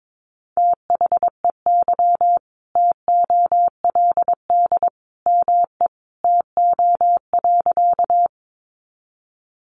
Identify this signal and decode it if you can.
Morse code, heard as a keyed tone, about 22 wpm, 700 Hz. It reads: THEY TOLD ME TO.
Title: Sentences from Top 400 Words 22wpm